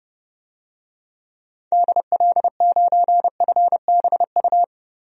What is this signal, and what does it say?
Morse code, heard as a keyed tone, about 30 wpm, 700 Hz. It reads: DL9FBU
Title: DXCC-100 Call Signs - 6 Characters - Repeated 3x - 30wpm